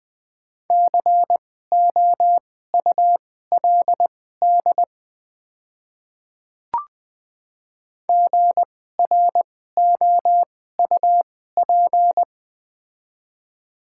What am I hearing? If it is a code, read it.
Morse code, heard as a keyed tone, about 20 wpm, spaced as for 14 wpm, 700 Hz. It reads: COULD GROUP
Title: 2x Character Spacing - Top 300 Words - 20wpm